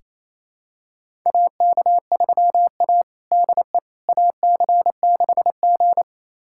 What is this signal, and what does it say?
Morse code, heard as a keyed tone, about 28 wpm, 700 Hz. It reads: AK3A DE AC6G
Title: USA Callsign DE Callsign - 4 Characters - 28wpm